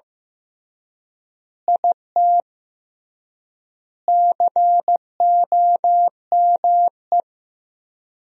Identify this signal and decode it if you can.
Morse code, heard as a keyed tone, about 15 wpm, 700 Hz. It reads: IT COME
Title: 3x Word-Spacing - Sentences from Top 100 Words - 15wpm